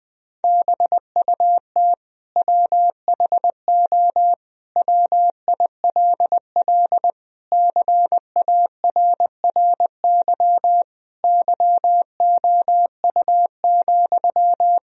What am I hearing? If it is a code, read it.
Morse code, heard as a keyed tone, about 20 wpm, 700 Hz. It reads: BUT WHO WILL CARRY YOU,
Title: Sentences from Top 1000 Words - Review Part 1 - 20wpm